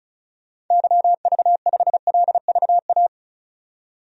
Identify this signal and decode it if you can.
Morse code, heard as a keyed tone, about 34 wpm, 700 Hz. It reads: YV5LVA
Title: DXCC-100 Call Signs - 6 Characters - 35wpm